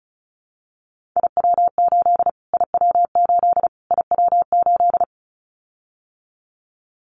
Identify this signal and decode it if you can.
Morse code, heard as a keyed tone, about 34 wpm, 700 Hz. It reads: IW8 IW8 IW8
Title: Callsign Prefixes for Top 25 DXCC - Repeated 3x - 35wpm